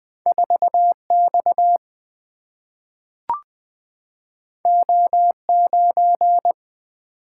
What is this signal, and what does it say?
Morse code, heard as a keyed tone, about 20 wpm, 700 Hz. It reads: 4X O9
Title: Single-Letter Single-Number 20wpm